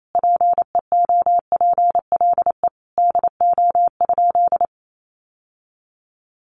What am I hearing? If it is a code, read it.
Morse code, heard as a keyed tone, about 28 wpm, 700 Hz. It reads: PEOPLE DO?
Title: Sentences from Top 1100 Words 28wpm